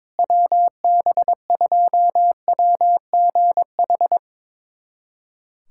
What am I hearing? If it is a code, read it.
Morse code, heard as a keyed tone, about 22 wpm, 700 Hz. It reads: WB2WGH
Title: USA Callsign DE Callsign - 6 Characters - 22wpm